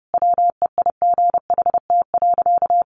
Morse code, heard as a keyed tone, about 28 wpm, 700 Hz. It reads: WEIGHT.
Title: Sentences from Top 800 Words 30wpm